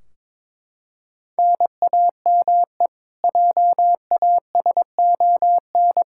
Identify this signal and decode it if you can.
Morse code, heard as a keyed tone, about 22 wpm, 700 Hz. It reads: NAME JASON
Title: K1USN SST QSO Practice 22wpm